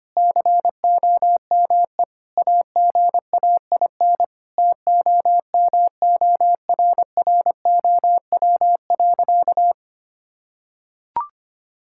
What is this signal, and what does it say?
Morse code, heard as a keyed tone, about 25 wpm, 700 Hz. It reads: COME AGAIN TOMORROW.
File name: practice / 25wpm / Sentences with 4-Word Phrases - 25wpm